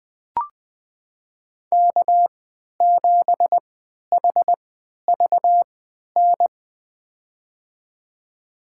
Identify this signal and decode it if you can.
Morse code, heard as a keyed tone, about 20 wpm, spaced as for 11 wpm, 700 Hz. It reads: K7HVN
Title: USA Call Signs - 3x Character Spacing - 20wpm